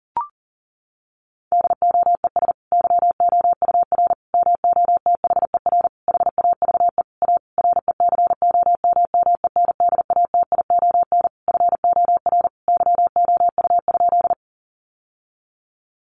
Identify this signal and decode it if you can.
Morse code, heard as a keyed tone, about 40 wpm, 700 Hz. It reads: DOES YOUR MOTHER HAVE A RECOMMENDATION FOR YOU?
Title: Sentences from Top 1400 Words 40wpm